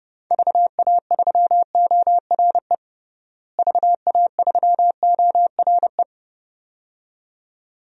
Morse code, heard as a keyed tone, about 32 wpm, 700 Hz. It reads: VA3ORE VA3ORE
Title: DXCC-100 Call Signs - 6 Characters - Repeated 2x - 30wpm